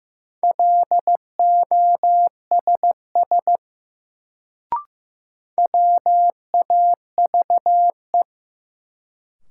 Morse code, heard as a keyed tone, about 15 wpm, 700 Hz. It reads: LOSS WAVE
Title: Sentences from Top 1400 Words - New Word Drill - 15wpm